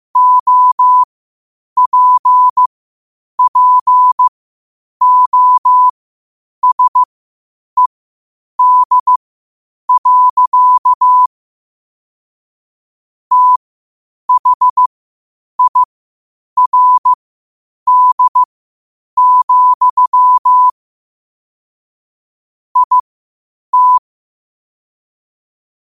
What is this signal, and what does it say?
Morse code, heard as a keyed tone, about 15 wpm, spaced as for 8 wpm, 1000 Hz. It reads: OPPOSED. THIRD, IT
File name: qotdfiles / 8wpm